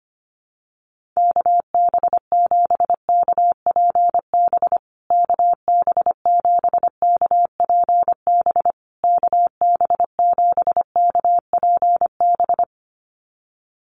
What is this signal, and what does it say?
Morse code, heard as a keyed tone, about 25 wpm, 700 Hz. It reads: KB7KPB KB7KPB KB7KPB
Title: USA Call Signs Repeated 3x 25wpm